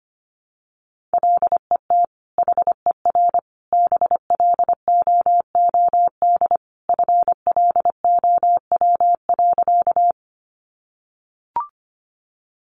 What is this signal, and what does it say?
Morse code, heard as a keyed tone, about 25 wpm, 700 Hz. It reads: LET HER BLOOD FLOW.
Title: Sentences from Top 1200 Words 25wpm